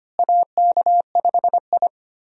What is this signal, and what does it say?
Morse code, heard as a keyed tone, about 25 wpm, 700 Hz. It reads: AK5I
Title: USA Call Signs - 4 Characters - Repeated 3x - 25wpm